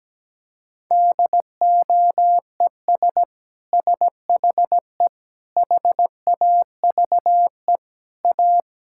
Morse code, heard as a keyed tone, about 17 wpm, 700 Hz. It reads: DOES SHE HAVE A
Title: Sentences from Top 1000 Words - Review Part 6 - 17wpm